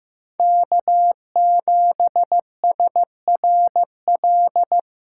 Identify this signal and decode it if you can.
Morse code, heard as a keyed tone, about 16 wpm, 700 Hz. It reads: K7SRL
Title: SOTA QSO Practice 15wpm